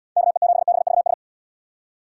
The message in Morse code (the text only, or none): KE7BVS